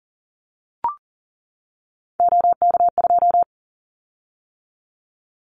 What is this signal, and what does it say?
Morse code, heard as a keyed tone, about 40 wpm, 700 Hz. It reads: OK2
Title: Callsign Prefixes for Top 100 DXCC - 40wpm